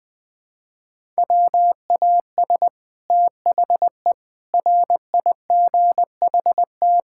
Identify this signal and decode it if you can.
Morse code, heard as a keyed tone, about 20 wpm, 700 Hz. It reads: WAS THE RIGHT
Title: Sets of 6 Words - 20wpm